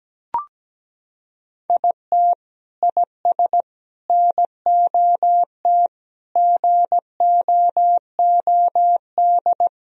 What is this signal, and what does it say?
Morse code, heard as a keyed tone, about 17 wpm, 700 Hz. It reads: IT IS NOT GOOD
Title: Sentences from Top 1000 Words - Review Part 2 - 17wpm